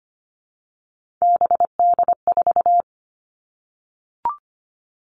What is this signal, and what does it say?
Morse code, heard as a keyed tone, about 25 wpm, 700 Hz. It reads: BD4
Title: All Cut Numbers - 3-Digit 25wpm